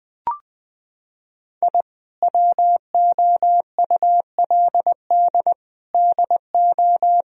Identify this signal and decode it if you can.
Morse code, heard as a keyed tone, about 20 wpm, 700 Hz. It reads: I WOULD DO
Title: Sentences from Top 1000 Words - Review Part 2 - 20wpm